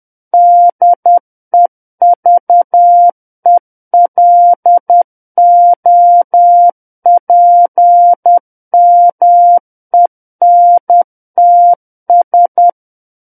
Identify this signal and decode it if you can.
Morse code, heard as a keyed tone, about 10 wpm, 700 Hz. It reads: DEVELOPMENTS
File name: Book One / 01-Chapter 1 - 10 WPM